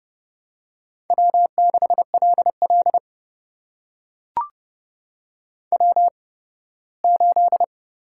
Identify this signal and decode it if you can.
Morse code, heard as a keyed tone, about 32 wpm, spaced as for 18 wpm, 700 Hz. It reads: W6LL W8
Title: USA Call Signs - 4 Characters - 8x Character Spacing - 30wpm